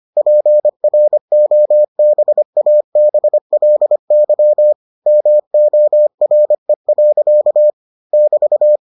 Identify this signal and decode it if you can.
Morse code, heard as a keyed tone, about 25 wpm, 600 Hz. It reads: PROBABLY MORE. <BT>